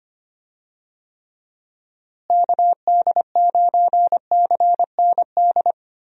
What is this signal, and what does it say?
Morse code, heard as a keyed tone, about 25 wpm, 700 Hz. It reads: KD9CND
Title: USA Call Signs - 6 Characters - Repeated 2x - 25wpm